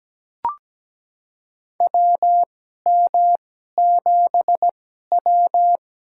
Morse code, heard as a keyed tone, about 17 wpm, spaced as for 12 wpm, 700 Hz. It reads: WM7W
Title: USA Call Signs - 4 Characters - 2x Character Spacing - 17wpm